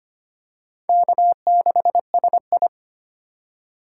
Morse code, heard as a keyed tone, about 25 wpm, 700 Hz. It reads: K6SI
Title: USA Call Signs - 4 Characters - 4x Character Spacing - 25wpm